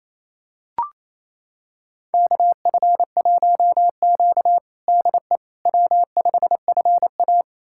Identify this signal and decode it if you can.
Morse code, heard as a keyed tone, about 28 wpm, 700 Hz. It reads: KF1Q DE W5FA
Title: USA Callsign DE Callsign - 4 Characters - 28wpm